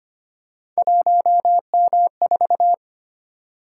Morse code, heard as a keyed tone, about 25 wpm, 700 Hz. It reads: 1M4
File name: Three Letter-Number Combinations - 25wpm